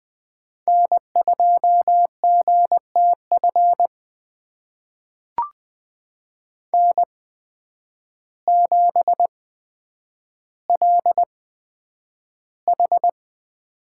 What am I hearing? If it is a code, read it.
Morse code, heard as a keyed tone, about 20 wpm, spaced as for 9 wpm, 700 Hz. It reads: N2GTF N7LH
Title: USA Call Signs - 5 Characters - 8x Character Spacing - 20wpm